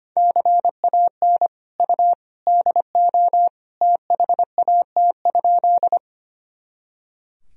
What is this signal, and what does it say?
Morse code, heard as a keyed tone, about 25 wpm, 700 Hz. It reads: CAN U DO THAT?
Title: Sentences from Top 100 Words and Abbreviations 25wpm